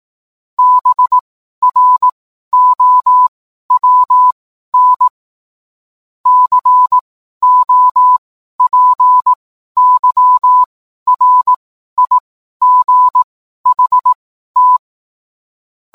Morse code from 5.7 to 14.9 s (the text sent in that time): COPYRIGHT